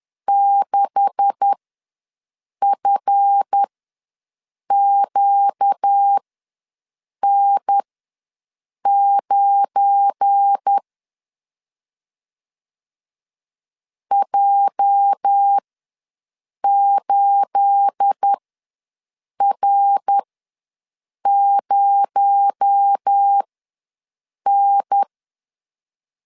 Morse code, heard as a keyed tone, about 10 wpm, spaced as for 6 wpm, 800 Hz. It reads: 6FQN9 J8R0N
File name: LEZIONI / 10 SIGNS / 10D COMA ws 10wpm 4314